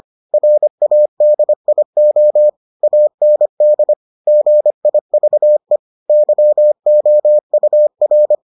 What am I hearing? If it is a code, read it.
Morse code, heard as a keyed tone, about 25 wpm, 600 Hz. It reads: RADIO AND GIVE YOUR